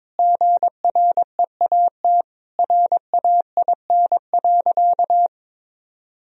Morse code, heard as a keyed tone, about 21 wpm, 700 Hz. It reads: GREAT RAIN.
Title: Sentences from Top 1000 Words - Review Part 3 - 22wpm